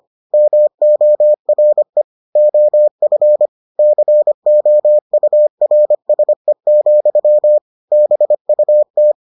Morse code, heard as a keyed tone, about 25 wpm, 600 Hz. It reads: MORE OF COURSE, BUT